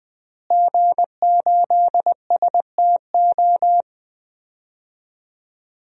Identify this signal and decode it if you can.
Morse code, heard as a keyed tone, about 21 wpm, 700 Hz. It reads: G8STO
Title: DXCC-100 Call Signs - 5 Characters - Repeated 3x - 20wpm